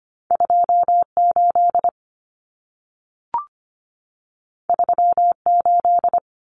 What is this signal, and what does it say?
Morse code, heard as a keyed tone, about 25 wpm, 700 Hz. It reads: 28 38
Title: Common Cut Numbers - 2-Digit 25wpm